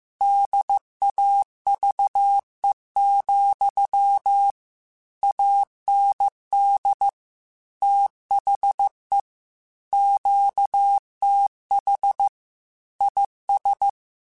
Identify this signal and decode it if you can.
Morse code, heard as a keyed tone, about 15 wpm, 800 Hz. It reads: DAVE, AND THE QTH IS
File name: QSOs / QSOs-mp3s / kc7sac@13